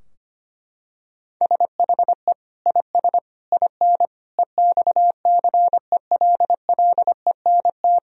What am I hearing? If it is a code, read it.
Morse code, heard as a keyed tone, about 25 wpm, 700 Hz. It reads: SHE IS IN EXCELLENT
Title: Sentences from Top 1000 Words - Review Part 3 - 25wpm